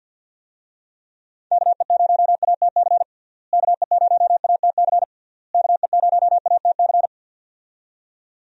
Morse code, heard as a keyed tone, about 46 wpm, 700 Hz. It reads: KE0ATC KE0ATC KE0ATC
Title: USA Call Signs - 6 Characters - Repeated 3x - 50wpm